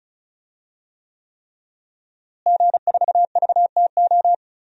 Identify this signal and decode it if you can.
Morse code, heard as a keyed tone, about 34 wpm, 700 Hz. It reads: G4VTO
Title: DXCC-100 Call Signs - 5 Characters - Repeated 3x - 35wpm